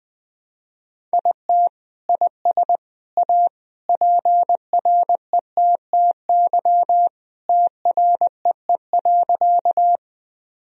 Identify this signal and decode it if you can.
Morse code, heard as a keyed tone, about 20 wpm, 700 Hz. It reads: IT IS A PRETTY TREE.
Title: Sentences from Top 1000 Words - Review Part 1 - 20wpm